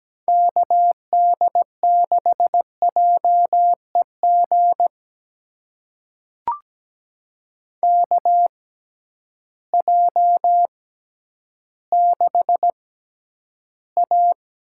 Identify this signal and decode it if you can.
Morse code, heard as a keyed tone, about 17 wpm, spaced as for 10 wpm, 700 Hz. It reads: KD6JEG KJ6A
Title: USA Call Signs - 6 Characters - 6x Character Spacing - 17wpm